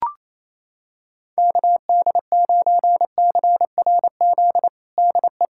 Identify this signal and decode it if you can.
Morse code, heard as a keyed tone, about 28 wpm, 700 Hz. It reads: KD9CRZ DE
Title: USA Callsign DE Callsign 28wpm